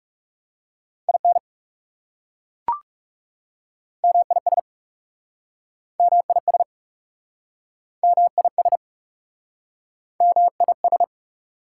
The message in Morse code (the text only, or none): IN MIS MIS MIS MIS